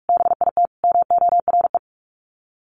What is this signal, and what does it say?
Morse code, heard as a keyed tone, about 44 wpm, 700 Hz. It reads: BIT MORE
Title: Sets of 4 Words 45wpm